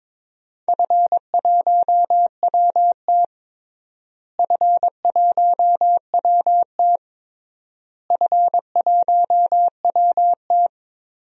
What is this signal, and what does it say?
Morse code, heard as a keyed tone, about 22 wpm, 700 Hz. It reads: F1WT F1WT F1WT
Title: DXCC-100 Call Signs - 4 Characters - Repeated 3x - 22wpm